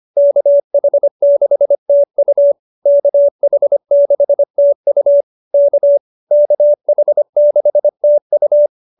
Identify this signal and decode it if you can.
Morse code, heard as a keyed tone, about 25 wpm, 590 Hz. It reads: KH6TU KH6TU K KH6TU